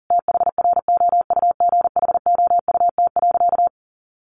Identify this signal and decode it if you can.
Morse code, heard as a keyed tone, about 37 wpm, 700 Hz. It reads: THROUGHOUT.
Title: Sentences from Top 800 Words 40wpm